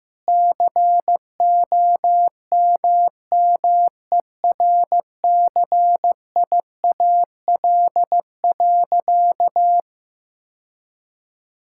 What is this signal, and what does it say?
Morse code, heard as a keyed tone, about 15 wpm, 700 Hz. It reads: COMMERCIAL.
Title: Sentences from Top 1000 Words - Review Part 5 - 15wpm